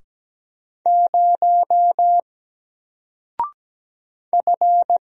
Morse code, heard as a keyed tone, about 17 wpm, 700 Hz. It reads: 0 F
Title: Single Character (Letter:Number:Punctuation:Prosign) 17wpm